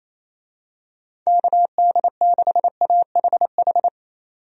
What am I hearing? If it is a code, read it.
Morse code, heard as a keyed tone, about 27 wpm, 700 Hz. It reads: KD6AHH
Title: USA Call Signs - 6 Characters - 28wpm